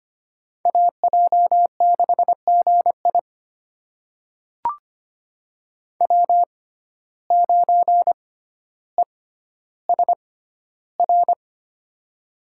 Morse code, heard as a keyed tone, about 25 wpm, spaced as for 13 wpm, 700 Hz. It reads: AJ6GI W9ESR
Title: USA Call Signs - 5 Characters - 6x Character Spacing - 25wpm